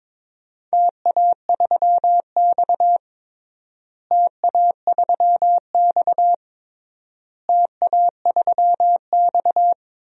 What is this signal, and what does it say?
Morse code, heard as a keyed tone, about 22 wpm, 700 Hz. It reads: TA3X TA3X TA3X
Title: DXCC-100 Call Signs - 4 Characters - Repeated 3x - 22wpm